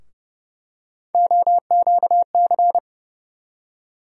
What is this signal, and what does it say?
Morse code, heard as a keyed tone, about 32 wpm, 700 Hz. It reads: OQC
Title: USA Call Sign Postfixes - 3 Letters 30wpm